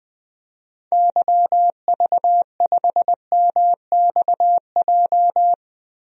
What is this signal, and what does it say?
Morse code, heard as a keyed tone, about 21 wpm, 700 Hz. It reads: YV5MXJ
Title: DXCC-100 Call Signs - 6 Characters - Repeated 2x - 20wpm